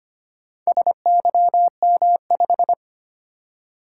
Morse code, heard as a keyed tone, about 25 wpm, 700 Hz. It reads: SYM5
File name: practice / 25wpm / Four Letter-Number Combinations - 25wpm